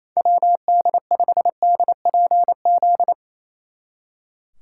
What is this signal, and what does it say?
Morse code, heard as a keyed tone, about 34 wpm, 700 Hz. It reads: WD5DPZ
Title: Speed Racing - USA 2x3 Call Signs 28wpm